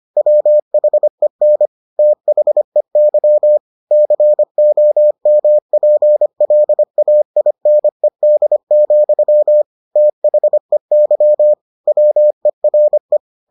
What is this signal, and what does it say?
Morse code, heard as a keyed tone, about 25 wpm, 600 Hz. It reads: WHEN THEY COMPLAINED, THEY WERE